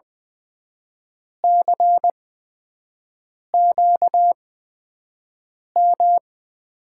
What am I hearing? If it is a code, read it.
Morse code, heard as a keyed tone, about 20 wpm, spaced as for 5 wpm, 700 Hz. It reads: CQM
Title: USA Call Signs - 8x Character Spacing - 20wpm